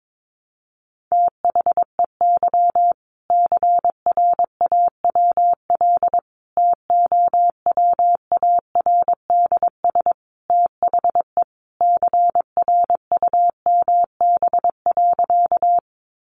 Morse code, heard as a keyed tone, about 22 wpm, 700 Hz. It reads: THEY CRAWL TOWARDS THE CRUMB.
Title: Easy Phrases 22wpm